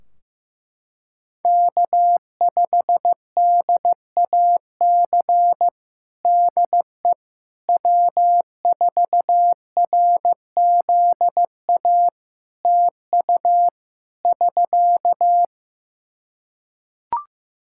Morse code, heard as a keyed tone, about 15 wpm, 700 Hz. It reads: K5DAC DE W4RZA TU <SK>
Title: Practice QSO - Exchange 4 - Ending - 15wpm